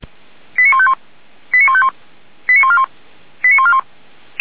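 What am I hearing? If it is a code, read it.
Five-tone selective call (ZVEI), heard as quick runs of five tones, tones of about 70 ms. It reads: 89151 89151 89141 89141
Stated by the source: Received on SDR; ZVEI five-tone selective call.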